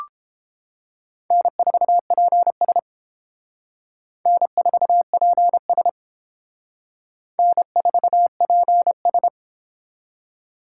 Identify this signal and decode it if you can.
Morse code, heard as a keyed tone, about 31 wpm, 700 Hz. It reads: N4PS N4PS N4PS